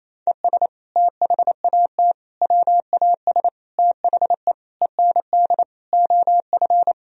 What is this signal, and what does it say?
Morse code, heard as a keyed tone, about 28 wpm, 700 Hz. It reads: ES THAT WAS THE END OF
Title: Sentences from Top 200 Words and Abbreviations 28wpm